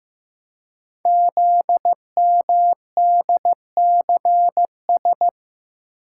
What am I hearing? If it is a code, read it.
Morse code, heard as a keyed tone, about 15 wpm, 700 Hz. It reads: ZMDCS
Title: Code Groups - 5 Letters - 15wpm